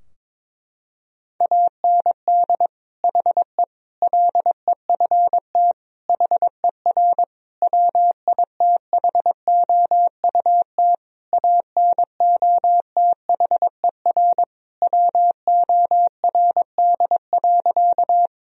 Morse code, heard as a keyed tone, about 22 wpm, 700 Hz. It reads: AND HE LEFT HER WITHOUT ANOTHER WORD.
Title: Sentences from Top 1000 Words - Review Part 3 - 22wpm